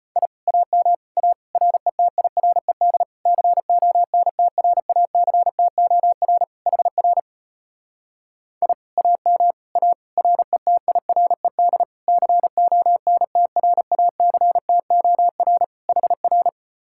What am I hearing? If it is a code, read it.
Morse code, heard as a keyed tone, about 33 wpm, 700 Hz. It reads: I AM A RETIRED CONTRACTOR HR I AM A RETIRED CONTRACTOR HR